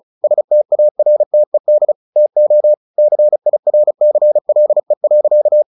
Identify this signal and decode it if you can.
Morse code, heard as a keyed tone, about 34 wpm, 600 Hz. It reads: STARTED TO CIRCLE.
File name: the old man and the sea 2 35wpm